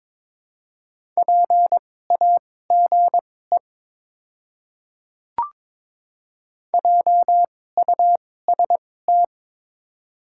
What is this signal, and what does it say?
Morse code, heard as a keyed tone, about 22 wpm, spaced as for 16 wpm, 700 Hz. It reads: PAGE JUST
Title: 2x Character Spacing - Top 300 Words - 22wpm